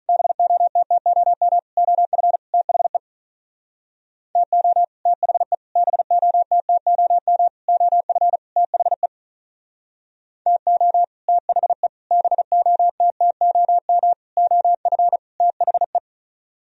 Morse code, heard as a keyed tone, about 45 wpm, 700 Hz. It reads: BOTTOM OF THE TO THE BOTTOM OF THE TO THE BOTTOM OF THE